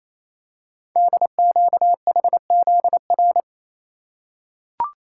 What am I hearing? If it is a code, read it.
Morse code, heard as a keyed tone, about 29 wpm, 700 Hz. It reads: DQHZR
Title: Code Groups - 5 Letter:Numbers - 28wpm